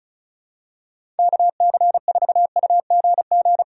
Morse code, heard as a keyed tone, about 34 wpm, 700 Hz. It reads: KC4UGG
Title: USA Call Signs - 6 Characters - 35wpm